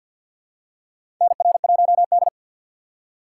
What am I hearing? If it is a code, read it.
Morse code, heard as a keyed tone, about 47 wpm, 700 Hz. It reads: NR1D
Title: USA Call Signs - 4 Characters - 3x Character Spacing - 50wpm